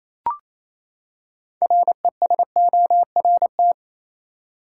Morse code, heard as a keyed tone, about 28 wpm, 700 Hz. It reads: RESORT
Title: Six Letter Words - 28wpm